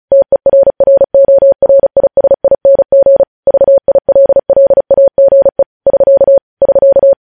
Morse code, heard as a keyed tone, about 36 wpm, 550 Hz. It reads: TERRORISING VILLAGE <SK> <SK>
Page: news headlines in morse code at 35 wpm